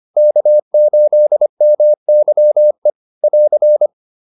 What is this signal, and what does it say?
Morse code, heard as a keyed tone, about 25 wpm, 600 Hz. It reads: K8MYE <AR>
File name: QSO403 25